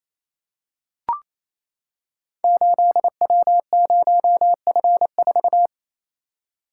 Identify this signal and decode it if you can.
Morse code, heard as a keyed tone, about 29 wpm, 700 Hz. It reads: 8W0F4
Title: Code Groups - 5 Letter:Numbers - 28wpm